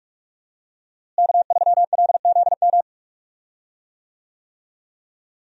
Morse code, heard as a keyed tone, about 46 wpm, 700 Hz. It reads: K3LZM
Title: USA Call Signs - 5 Characters - Repeated 3x - 45wpm